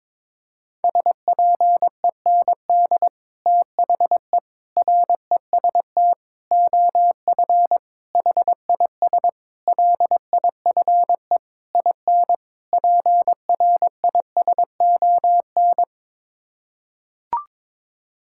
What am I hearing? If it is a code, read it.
Morse code, heard as a keyed tone, about 22 wpm, 700 Hz. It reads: SPEND THE REST OF HIS LIFE IN PRISON
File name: practice / 22wpm / Sets of 8 Words - 22wpm